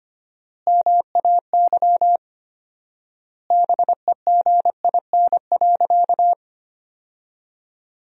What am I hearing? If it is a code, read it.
Morse code, heard as a keyed tone, about 25 wpm, 700 Hz. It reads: MAY BEGIN.
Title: 4x Word-Spacing - Sentences from Top 300 Words - 25wpm